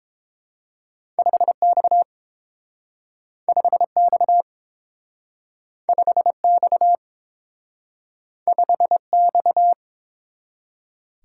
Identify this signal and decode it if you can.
Morse code, heard as a keyed tone, about 30 wpm, 700 Hz. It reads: 5X 5X 5X 5X